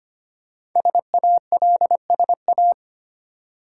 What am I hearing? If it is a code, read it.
Morse code, heard as a keyed tone, about 25 wpm, 700 Hz. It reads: SALSA
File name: practice / 25wpm / Easy Words with Hints - 25wpm